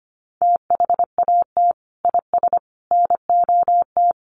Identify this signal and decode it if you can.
Morse code, heard as a keyed tone, about 25 wpm, 700 Hz. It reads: THAT IS NOT
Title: Sentences from Top 1300 Words 25wpm